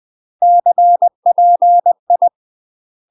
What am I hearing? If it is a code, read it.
Morse code, heard as a keyed tone, about 20 wpm, 700 Hz. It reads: CPI
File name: UK MASTER QSOs TRAINING / 20WPM / UK MASTER QSOs 0063